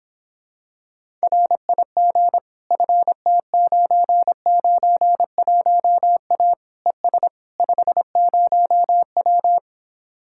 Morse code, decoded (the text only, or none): RIG FT991A ES 50W